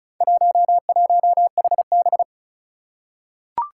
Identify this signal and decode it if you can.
Morse code, heard as a keyed tone, about 36 wpm, 700 Hz. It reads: 11HB
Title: Four Challenging Letter-Number Combinations 35wpm